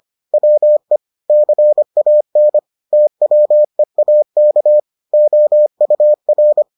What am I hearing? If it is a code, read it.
Morse code, heard as a keyed tone, about 25 wpm, 600 Hz. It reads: WE CAN TWEAK OUR